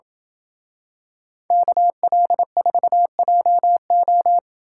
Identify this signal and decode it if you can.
Morse code, heard as a keyed tone, about 27 wpm, 700 Hz. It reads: KL4JO